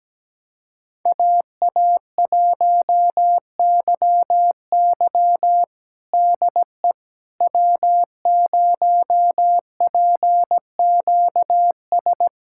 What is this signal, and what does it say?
Morse code, heard as a keyed tone, about 17 wpm, 700 Hz. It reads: AA1YY DE W0PQS
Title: USA Callsign DE Callsign - 5 Characters - 17wpm